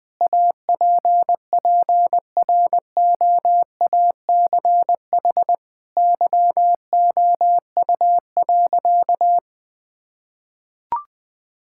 Morse code, heard as a keyed tone, about 20 wpm, 700 Hz. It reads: APPROACH YOU.
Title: Sentences from Top 1000 Words - Review Part 6 - 20wpm